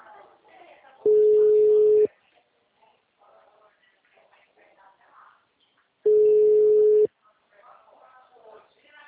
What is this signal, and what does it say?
Ringback tone (Europe, 425 Hz), with speech.
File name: out- -2003- - - .71